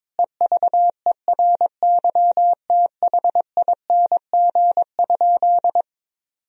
Morse code, heard as a keyed tone, about 22 wpm, 700 Hz. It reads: EVERYTHING?
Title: Sentences from Top 1000 Words - Review Part 2 - 22wpm